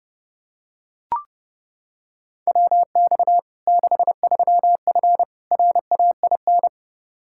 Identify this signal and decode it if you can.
Morse code, heard as a keyed tone, about 30 wpm, 700 Hz. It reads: WX 63F RAIN
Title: Practice QSO - Exchange 2 - Info - 30wpm